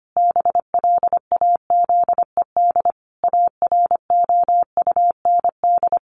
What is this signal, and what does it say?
Morse code, heard as a keyed tone, about 25 wpm, 700 Hz. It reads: BLAZED AROUND
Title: A Sound Of Thunder 25wpm